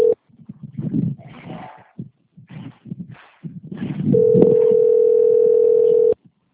Ringback tone (North America), 440 + 480 Hz.